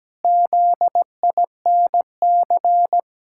Morse code, heard as a keyed tone, about 17 wpm, 700 Hz. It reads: ZINC
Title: Pangram Challenge v2 - 17wpm